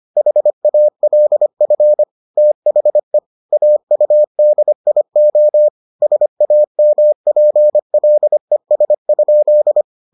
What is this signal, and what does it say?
Morse code, heard as a keyed tone, about 25 wpm, 600 Hz. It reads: HALF THE AUDIO SAMPLES?